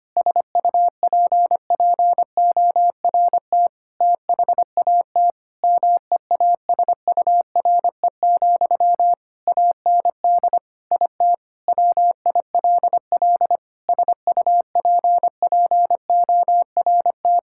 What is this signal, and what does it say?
Morse code, heard as a keyed tone, about 25 wpm, 700 Hz. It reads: SUPPORT THAT MEASURE, AND IT WILL SUPPORT
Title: Sentences from Top 1000 Words - Review Part 5 - 25wpm